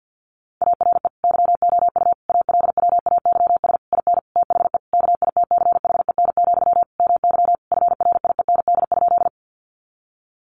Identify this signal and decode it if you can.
Morse code, heard as a keyed tone, about 50 wpm, 700 Hz. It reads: ARE YOU ALWAYS IN THE KITCHEN, MY FRIEND?
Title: Sentences from Top 1400 Words 50wpm